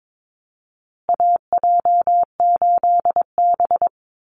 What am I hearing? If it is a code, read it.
Morse code, heard as a keyed tone, about 22 wpm, 700 Hz. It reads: AJ8B
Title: Call Signs Repeated 2x - 22wpm